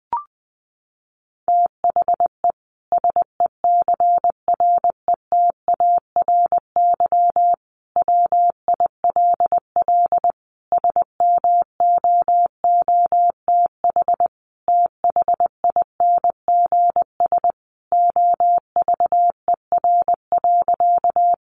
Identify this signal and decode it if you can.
Morse code, heard as a keyed tone, about 20 wpm, 700 Hz. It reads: THE SECRETARY WILL SMOOTH THINGS OVER.
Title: Sentences from Top 1400 Words 20wpm